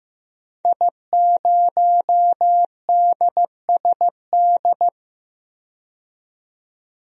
Morse code, heard as a keyed tone, about 15 wpm, 700 Hz. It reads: I0DSD
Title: DXCC-100 Call Signs - 5 Characters - 15wpm